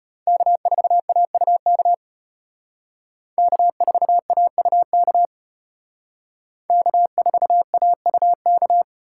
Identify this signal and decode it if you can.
Morse code, heard as a keyed tone, about 33 wpm, 700 Hz. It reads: K4AUK K4AUK K4AUK